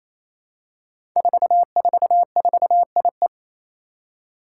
Morse code, heard as a keyed tone, about 27 wpm, 700 Hz. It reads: 444IE
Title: Five Challenging Letter-Number Combinations 28wpm